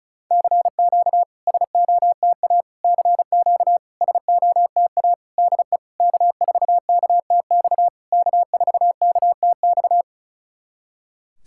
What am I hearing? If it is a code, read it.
Morse code, heard as a keyed tone, about 36 wpm, 700 Hz. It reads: CQ SOTA CQ SOTA DE K4KTX K4KTX
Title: SOTA QSO Practice 35wpm